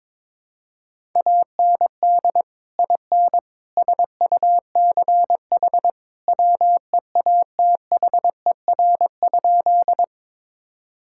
Morse code, heard as a keyed tone, about 22 wpm, 700 Hz. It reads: AND IN SUCH WEATHER?
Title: Sentences from Top 1000 Words - Review Part 5 - 22wpm